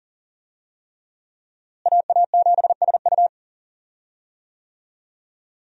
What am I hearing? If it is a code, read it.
Morse code, heard as a keyed tone, about 40 wpm, 700 Hz. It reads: AA7SU
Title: USA Call Signs - 5 Characters - Repeated 3x - 40wpm